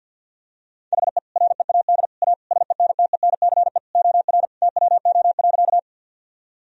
Morse code, heard as a keyed tone, about 52 wpm, 700 Hz. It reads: HE READ A SENTENCE OR TWO.
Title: Sentences from Top 1000 Words - Review Part 1 - 50wpm